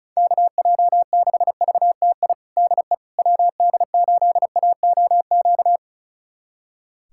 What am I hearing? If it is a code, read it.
Morse code, heard as a keyed tone, about 34 wpm, 700 Hz. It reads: KJ6VTI DE WD8AOQ
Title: USA Callsign DE Callsign - 6 Characters - 35wpm